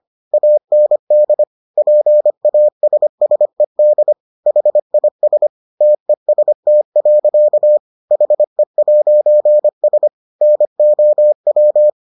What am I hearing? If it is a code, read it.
Morse code, heard as a keyed tone, about 25 wpm, 600 Hz. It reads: AND PASSED HIS TEST. HE'S NOW